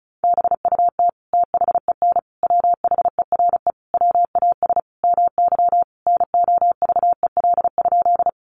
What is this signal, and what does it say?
Morse code, heard as a keyed tone, about 34 wpm, 700 Hz. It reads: BUT THEN WHERE WAS MY NOVEL?
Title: Sentences from Top 1300 Words 35wpm